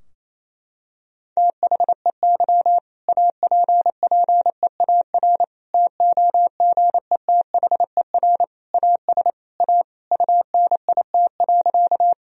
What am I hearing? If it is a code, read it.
Morse code, heard as a keyed tone, about 28 wpm, 700 Hz. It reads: THEY APPEAR TOGETHER AS A UNIT.
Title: Sentences from Top 1000 Words - Review Part 2 - 28wpm